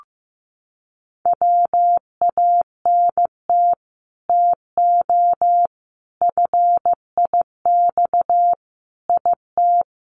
Morse code, heard as a keyed tone, about 15 wpm, 700 Hz. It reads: WANT TO FIX IT
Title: Sentences from Top 1400 Words 15wpm